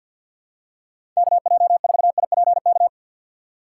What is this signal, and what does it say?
Morse code, heard as a keyed tone, about 52 wpm, 700 Hz. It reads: KJ4IPK
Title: USA Call Signs - 6x Character Spacing - 50wpm